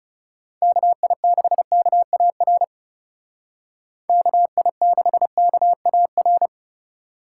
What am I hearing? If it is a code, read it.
Morse code, heard as a keyed tone, about 37 wpm, 700 Hz. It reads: KI6KAR KI6KAR
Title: Speed Racing - USA 2x3 Call Signs 30wpm